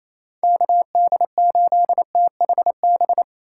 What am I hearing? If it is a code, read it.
Morse code, heard as a keyed tone, about 28 wpm, 700 Hz. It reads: KD8THB
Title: USA Call Signs - 6 Characters - 4x Character Spacing - 28wpm